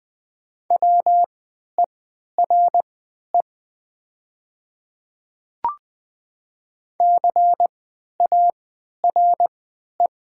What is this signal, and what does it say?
Morse code, heard as a keyed tone, about 20 wpm, spaced as for 11 wpm, 700 Hz. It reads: WERE CARE
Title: 3x Character Spacing - Top 300 Words - 20wpm